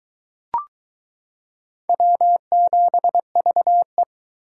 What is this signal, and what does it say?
Morse code, heard as a keyed tone, about 22 wpm, 700 Hz. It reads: W7VE